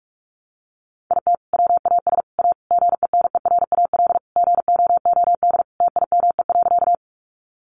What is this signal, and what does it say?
Morse code, heard as a keyed tone, about 44 wpm, 700 Hz. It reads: IT WAS A GENERAL GOOD TIME.
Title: Sentences from Top 600 Words 45wpm